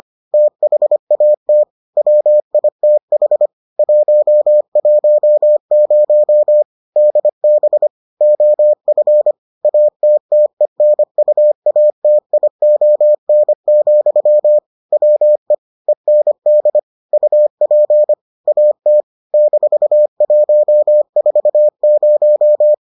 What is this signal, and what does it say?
Morse code, heard as a keyed tone, about 25 wpm, 600 Hz. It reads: THAT WITH 110 DB OF ATTENUATION, WE END UP AT -140